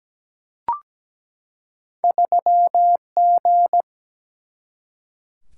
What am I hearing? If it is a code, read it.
Morse code, heard as a keyed tone, about 17 wpm, 700 Hz. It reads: 3G
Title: Single-Letter Single-Number 17wpm